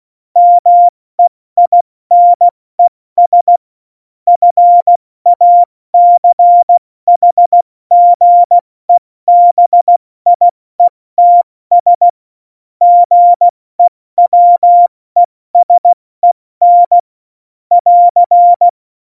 Morse code, heard as a keyed tone, about 16 wpm, 700 Hz. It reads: MEINES FACHGEBIETS GEWESEN <AR>
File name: Morsekurs / Witze / Witz 29. September 2020, 70 F80